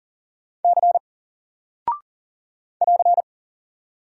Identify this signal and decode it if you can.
Morse code, heard as a keyed tone, about 40 wpm, 700 Hz. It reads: C <AR>
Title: Single Character (Letter:Number:Punctuation:Prosign) Alt 40wpm